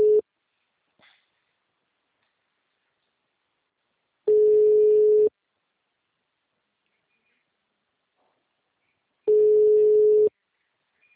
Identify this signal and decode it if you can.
Ringback tone (Europe, 425 Hz).
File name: out- -2036- - - .3120